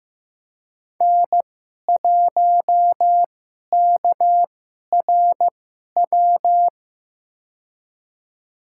Morse code, heard as a keyed tone, about 15 wpm, spaced as for 11 wpm, 700 Hz. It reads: N1KRW
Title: USA Call Signs - 5 Characters - 2x Character Spacing - 15wpm